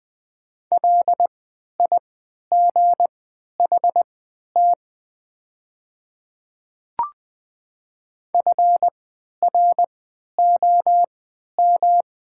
Morse code, heard as a keyed tone, about 20 wpm, spaced as for 11 wpm, 700 Hz. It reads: LIGHT FROM
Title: 3x Character Spacing - Top 200 Words - 20wpm